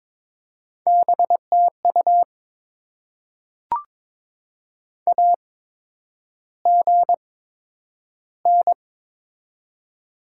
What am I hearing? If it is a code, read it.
Morse code, heard as a keyed tone, about 23 wpm, spaced as for 10 wpm, 700 Hz. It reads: BTU AGN
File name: Top 20 QSO Elements - 8x Character Spacing - 22wpm